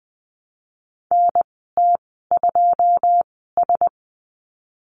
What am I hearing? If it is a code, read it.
Morse code, heard as a keyed tone, about 20 wpm, spaced as for 15 wpm, 700 Hz. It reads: NT2S
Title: 161 Lesson 26g - Letter g - Call Signs 20wpm